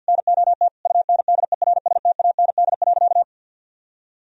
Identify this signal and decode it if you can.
Morse code, heard as a keyed tone, about 46 wpm, 700 Hz. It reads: NOT UNDERSTAND.
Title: Sentences from Top 1000 Words - Review Part 1 - 50wpm